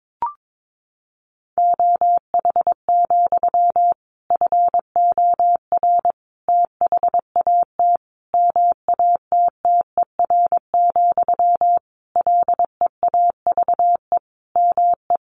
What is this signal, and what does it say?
Morse code, heard as a keyed tone, about 22 wpm, 700 Hz. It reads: OH, FOR THAT MATTER, LEAVE ME
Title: Sentences from Top 1400 Words 22wpm